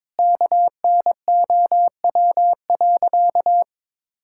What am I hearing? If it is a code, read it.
Morse code, heard as a keyed tone, about 22 wpm, 700 Hz. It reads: KNOW.
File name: practice / 22wpm / Sentences with 3-Word Phrases - 22wpm